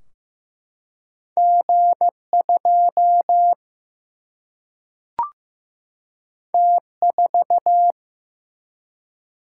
Morse code, heard as a keyed tone, about 15 wpm, 700 Hz. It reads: G2 T4
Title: Single-Letter Single-Number 15wpm